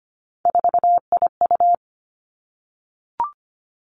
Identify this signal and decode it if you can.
Morse code, heard as a keyed tone, about 25 wpm, 700 Hz. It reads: 4IU
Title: Three Challenging Letter-Number Combinations 25wpm